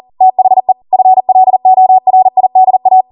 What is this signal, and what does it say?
Morse code, heard as a keyed tone, about 40 wpm, 750 Hz. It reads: THE FLORIDA